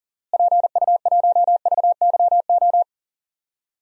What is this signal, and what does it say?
Morse code, heard as a keyed tone, about 40 wpm, 700 Hz. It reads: PU1VYO
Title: DXCC-100 Call Signs - 6 Characters - Repeated 2x - 40wpm